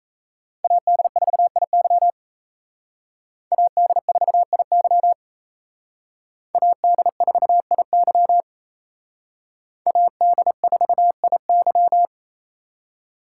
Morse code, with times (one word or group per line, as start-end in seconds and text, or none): 0.6-2.1 AD4IY
3.5-5.1 AD4IY
6.5-8.4 AD4IY
9.9-12.1 AD4IY